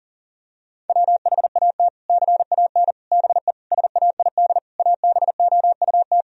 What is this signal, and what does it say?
Morse code, heard as a keyed tone, about 40 wpm, 700 Hz. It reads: WHAT CAN BE SAID ABOUT
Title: Sentences from Top 1000 Words - Review Part 4 - 40wpm